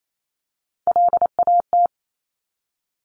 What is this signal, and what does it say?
Morse code, heard as a keyed tone, about 28 wpm, 690 Hz. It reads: LAT
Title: Top Three Letter Combinations in Words 28wpm